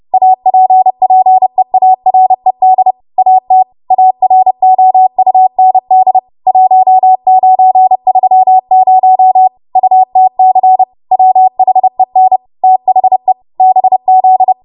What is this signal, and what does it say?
Morse code, heard as a keyed tone, about 31 wpm, 750 Hz. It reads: APPEARED AT AROUND 1930 UTC WHEN THE BZ
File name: Morse / 30WPM